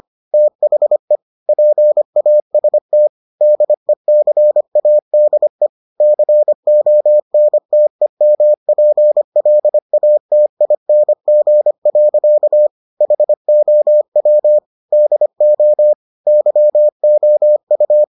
Morse code, heard as a keyed tone, about 25 wpm, 600 Hz. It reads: THE PAST DECADE CONTEMPLATING. HOW DO YOU